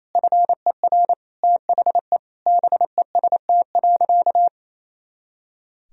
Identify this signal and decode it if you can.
Morse code, heard as a keyed tone, about 28 wpm, 700 Hz. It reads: FER THE BEST.
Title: Sentences from Top 500 Words and Abbreviations 28wpm